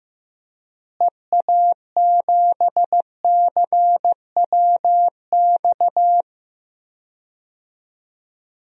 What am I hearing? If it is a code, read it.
Morse code, heard as a keyed tone, about 15 wpm, 700 Hz. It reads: EA7CWX
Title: DXCC-100 Call Signs - 6 Characters - Repeated 2x - 15wpm